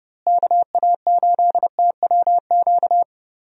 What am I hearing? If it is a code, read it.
Morse code, heard as a keyed tone, about 30 wpm, 700 Hz. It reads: KA8TWQ
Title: USA Call Signs - 6 Characters - Repeated 3x - 30wpm